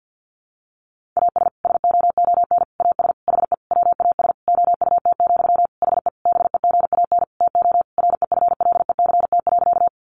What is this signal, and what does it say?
Morse code, heard as a keyed tone, about 50 wpm, 700 Hz. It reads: AS SOON AS HE WAS OUT, HE BEGAN TO REFLECT.
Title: Sentences from Top 1500 Words 50wpm